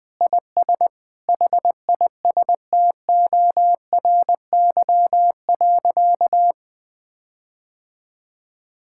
Morse code, heard as a keyed tone, about 21 wpm, 700 Hz. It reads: IS HISTORY.
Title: Sentences with 8-Word Phrases - 20wpm